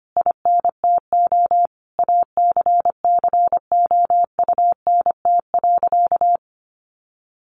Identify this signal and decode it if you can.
Morse code, heard as a keyed tone, about 25 wpm, 700 Hz. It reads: INTO ACCOUNT.
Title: Sentences from Top 1300 Words 25wpm